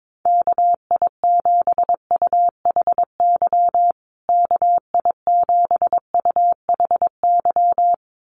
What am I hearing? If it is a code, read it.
Morse code, heard as a keyed tone, about 22 wpm, 700 Hz. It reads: KI7UHY KI7UHY
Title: USA Call Signs Repeated 2x 22wpm